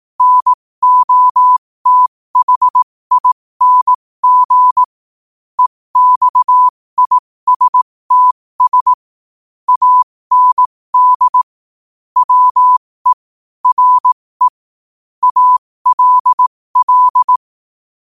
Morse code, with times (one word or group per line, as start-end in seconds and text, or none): 0.2-4.8 NOTHING
5.6-8.9 EXISTS
9.7-11.4 AND
12.2-13.1 WE
13.7-14.5 RE
15.2-17.4 ALL